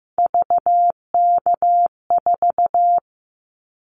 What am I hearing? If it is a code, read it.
Morse code, heard as a keyed tone, about 15 wpm, 700 Hz. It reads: VK4
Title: Callsign Prefixes for Top 100 DXCC - 15wpm